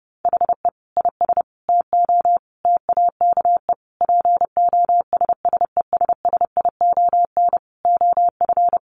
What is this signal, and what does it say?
Morse code, heard as a keyed tone, about 30 wpm, 700 Hz. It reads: HE IS TO TAKE POSSESSION OF
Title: Sentences from Top 1400 Words 30wpm